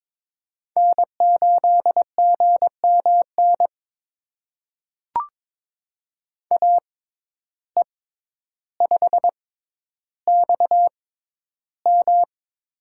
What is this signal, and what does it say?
Morse code, heard as a keyed tone, about 22 wpm, spaced as for 11 wpm, 700 Hz. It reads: N8GMN AE5XM
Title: USA Call Signs - 5 Characters - 6x Character Spacing - 22wpm